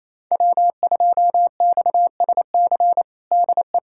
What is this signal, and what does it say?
Morse code, heard as a keyed tone, about 28 wpm, 700 Hz. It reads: W2XSC DE
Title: USA Callsign DE Callsign - 5 Characters - 28wpm